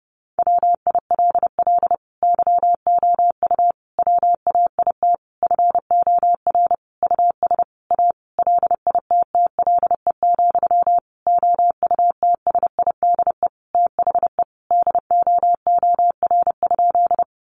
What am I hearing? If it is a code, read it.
Morse code, heard as a keyed tone, about 30 wpm, 700 Hz. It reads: WILL YOU WAIT FOR US A LITTLE, OUTSIDE THE DOOR?
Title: Sentences from Top 900 Words 30wpm